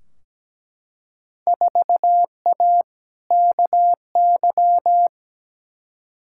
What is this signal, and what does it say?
Morse code, heard as a keyed tone, about 17 wpm, 700 Hz. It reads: 4A KY
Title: Field Day Exchanges Encore - 17wpm